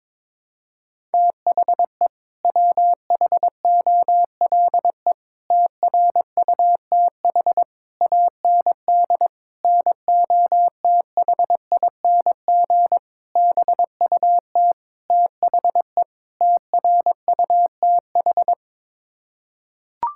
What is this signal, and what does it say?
Morse code, heard as a keyed tone, about 22 wpm, 700 Hz. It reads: THE WHOLE TRUTH AND NOTHING BUT THE TRUTH
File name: practice / 22wpm / Sets of 8 Words - 22wpm